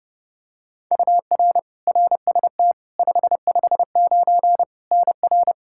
Morse code, heard as a keyed tone, about 30 wpm, 700 Hz. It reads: UR RST 559 NR
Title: K1USN SST QSO Practice 30wpm